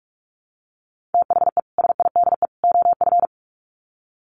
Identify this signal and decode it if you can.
Morse code, heard as a keyed tone, about 44 wpm, 700 Hz. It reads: THE SIDE OF
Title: Sets of 3 Words 45wpm